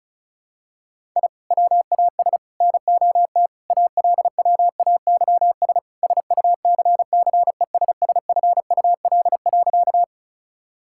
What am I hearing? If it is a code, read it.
Morse code, heard as a keyed tone, about 36 wpm, 700 Hz. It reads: I WAS NOT ALWAYS SUCCESSFUL.
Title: Sentences from Top 1000 Words - Review Part 6 - 35wpm